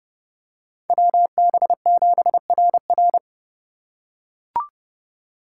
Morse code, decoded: WB7RR

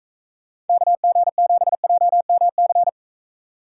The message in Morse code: KG7JMC

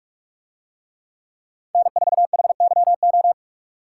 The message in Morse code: N4HYO

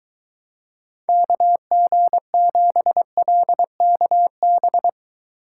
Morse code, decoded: KG7LKB

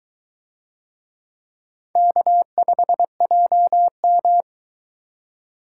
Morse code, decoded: K5JM

